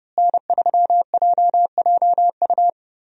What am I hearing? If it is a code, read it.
Morse code, heard as a keyed tone, about 30 wpm, 700 Hz. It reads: N3JJU
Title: USA Call Signs - 5 Characters - Repeated 2x - 30wpm